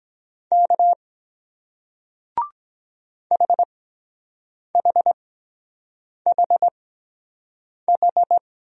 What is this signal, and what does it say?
Morse code, heard as a keyed tone, about 23 wpm, spaced as for 7 wpm, 700 Hz. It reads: K HHHH